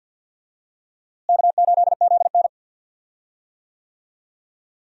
Morse code, heard as a keyed tone, about 52 wpm, 700 Hz. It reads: K8ZN
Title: USA Call Signs - 4 Characters - Repeated 3x - 50wpm